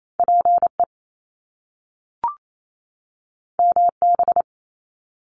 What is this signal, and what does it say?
Morse code, heard as a keyed tone, about 28 wpm, 700 Hz. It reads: PE MB
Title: Canadian Provinces and Territories 28wpm